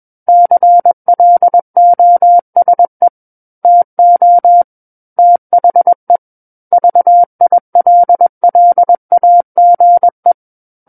Morse code, heard as a keyed tone, about 21 wpm, 700 Hz. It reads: CLOSE TO THE VILLAGE
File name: Book One / 12-Chapter 12 - 21 WPM